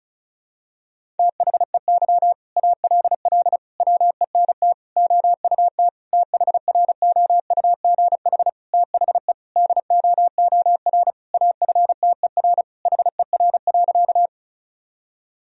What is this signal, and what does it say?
Morse code, heard as a keyed tone, about 36 wpm, 700 Hz. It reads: THEY ALL WENT OUT THROUGH THE DOOR AFTER HER.
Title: Sentences from Top 1000 Words - Review Part 1 - 35wpm